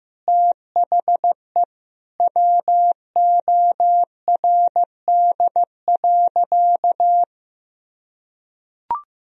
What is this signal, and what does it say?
Morse code, heard as a keyed tone, about 15 wpm, 700 Hz. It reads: THE WORD.
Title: Sentences from Top 1000 Words - Review Part 4 - 15wpm